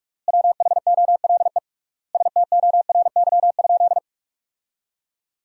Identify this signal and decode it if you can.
Morse code, heard as a keyed tone, about 46 wpm, 700 Hz. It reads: WHOLE STORY?
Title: 3x Word-Spacing - Sentences from Top 400 Words - 45wpm